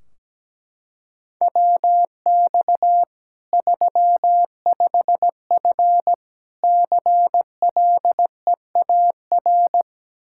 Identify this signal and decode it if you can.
Morse code, heard as a keyed tone, about 17 wpm, 700 Hz. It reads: WX 35F CLEAR
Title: K1USN SST QSO Practice 17wpm